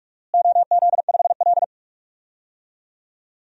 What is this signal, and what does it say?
Morse code, heard as a keyed tone, about 44 wpm, 700 Hz. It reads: OZ5L
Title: DXCC-100 Call Signs - 4 Characters - Repeated 3x - 45wpm